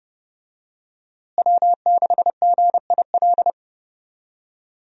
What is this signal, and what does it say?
Morse code, heard as a keyed tone, about 30 wpm, 700 Hz. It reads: W6GIL
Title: USA Call Signs - 5 Characters - Repeated 2x - 30wpm